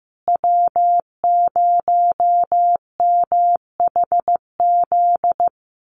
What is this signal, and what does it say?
Morse code, heard as a keyed tone, about 15 wpm, 700 Hz. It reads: W0MHZ
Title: USA Call Signs Repeated 3x 15wpm